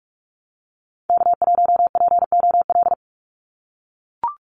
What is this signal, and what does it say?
Morse code, heard as a keyed tone, about 44 wpm, 700 Hz. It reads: K1POL
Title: USA Call Signs 45wpm